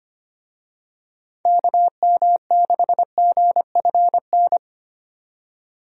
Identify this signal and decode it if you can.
Morse code, heard as a keyed tone, about 25 wpm, 700 Hz. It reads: KM6GFN
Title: USA Call Signs - 6 Characters - 25wpm